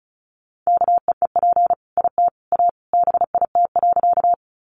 Morse code, heard as a keyed tone, about 34 wpm, 700 Hz. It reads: KEEP IT A BIT.
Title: Sentences from Top 900 Words 35wpm